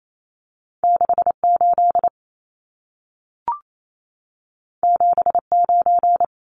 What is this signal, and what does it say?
Morse code, heard as a keyed tone, about 28 wpm, 700 Hz. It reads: 68 79
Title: Numbers - 2-Digit 28wpm